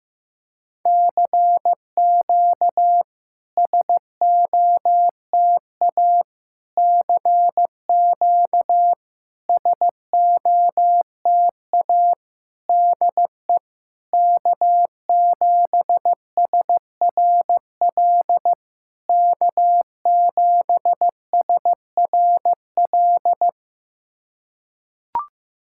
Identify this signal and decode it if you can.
Morse code, heard as a keyed tone, about 15 wpm, 660 Hz. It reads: CQ SOTA CQ SOTA DE K7SRL K7SRL
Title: SOTA QSO Practice 15wpm